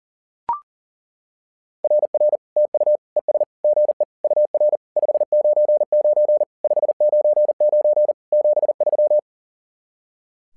Morse code, heard as a keyed tone, about 40 wpm, 600 Hz. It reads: RR TU ES GE UR 599 599 73
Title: SOTA QSO Practice 40wpm